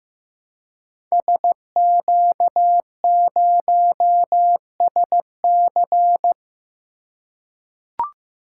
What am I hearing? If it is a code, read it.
Morse code, heard as a keyed tone, about 15 wpm, 700 Hz. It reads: SQ0SC
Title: Code Groups - 5 Letter:Numbers - 15wpm